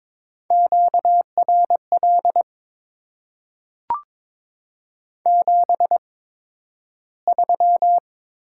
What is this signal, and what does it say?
Morse code, heard as a keyed tone, about 22 wpm, spaced as for 12 wpm, 700 Hz. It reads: QRL 73
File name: Top 20 QSO Elements - 8x Character Spacing - 22wpm